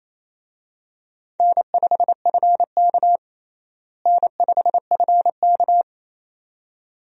Morse code, heard as a keyed tone, about 27 wpm, 700 Hz. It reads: N5FK N5FK
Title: USA Call Signs - 4 Characters - Repeated 2x - 28wpm